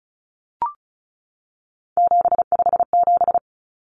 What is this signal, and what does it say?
Morse code, heard as a keyed tone, about 34 wpm, 700 Hz. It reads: 757
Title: Common Cut Numbers - 3-Digit 35wpm